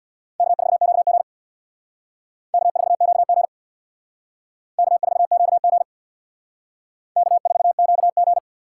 Morse code, heard as a keyed tone, about 60 wpm, 700 Hz. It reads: K4QG K4QG K4QG K4QG
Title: Speed Racing - USA 1x2 Call Signs 50wpm